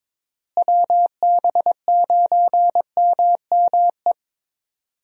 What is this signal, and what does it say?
Morse code, heard as a keyed tone, about 21 wpm, 700 Hz. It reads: WB9MME
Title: USA Call Signs - 6 Characters - Repeated 3x - 22wpm